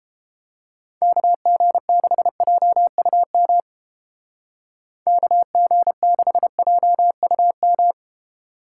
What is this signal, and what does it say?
Morse code, heard as a keyed tone, about 30 wpm, 700 Hz. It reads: KG6JUM KG6JUM